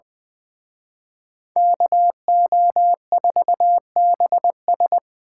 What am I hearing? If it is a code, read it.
Morse code, heard as a keyed tone, about 24 wpm, 700 Hz. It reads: KO4BS
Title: Speed Racing - USA 2x2 Call Signs 20wpm